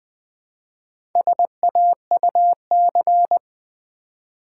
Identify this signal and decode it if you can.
Morse code, heard as a keyed tone, about 20 wpm, 700 Hz. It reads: SAUC
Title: Build A Word - 1 Letter At A Time - 5-Letter Words - 20wpm